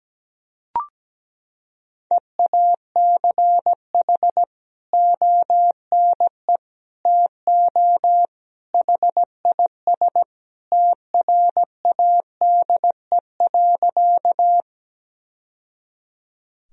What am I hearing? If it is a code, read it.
Morse code, heard as a keyed tone, about 17 wpm, 700 Hz. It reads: EACH ONE TO HIS TRADE.
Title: Sentences from Top 1000 Words - Review Part 4 - 17wpm